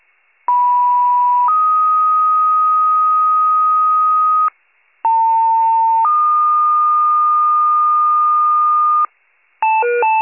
Two-tone page, heard as a one-second tone, then a three-second tone, about 980 then 1250 Hz, about 900 then 1220 Hz. It is heard from 0.5 to 9.1 s.